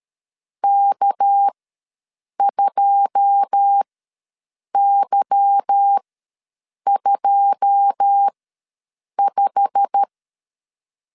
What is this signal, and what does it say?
Morse code, heard as a keyed tone, about 13 wpm, spaced as for 7 wpm, 800 Hz. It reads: K2Y25
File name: LEZIONI / 9 2 ws 12wpm 3406